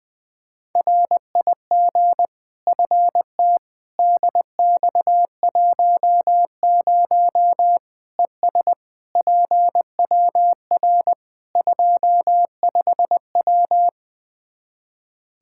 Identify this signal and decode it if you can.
Morse code, heard as a keyed tone, about 21 wpm, 700 Hz. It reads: RIG FT DX10 ES PWR 25W
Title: Practice QSO - Exchange 2 - Info - 20wpm